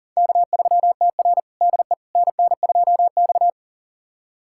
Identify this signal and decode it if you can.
Morse code, heard as a keyed tone, about 40 wpm, 700 Hz. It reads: K3TR DE NN2X
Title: USA Callsign DE Callsign - 4 Characters - 40wpm